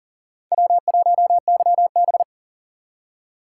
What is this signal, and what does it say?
Morse code, heard as a keyed tone, about 40 wpm, 700 Hz. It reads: W1YB
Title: USA Call Signs - 4 Characters - 6x Character Spacing - 40wpm